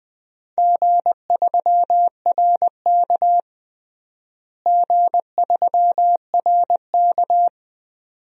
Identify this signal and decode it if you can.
Morse code, heard as a keyed tone, about 20 wpm, 700 Hz. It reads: G3RK G3RK
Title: DXCC-100 Call Signs - 4 Characters - Repeated 2x - 20wpm